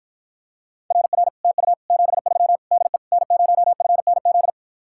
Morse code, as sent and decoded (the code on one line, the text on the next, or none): .-. .-. / - ..- / --... ...-- / -.. / -. ----- ..-. -. --..
RR TU 73 DE N0FNZ